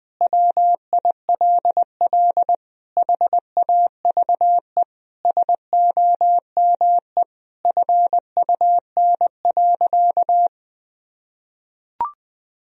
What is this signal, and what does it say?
Morse code, heard as a keyed tone, about 21 wpm, 700 Hz. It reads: WILL HAVE SOME FUN.
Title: Sentences from Top 1000 Words - Review Part 4 - 20wpm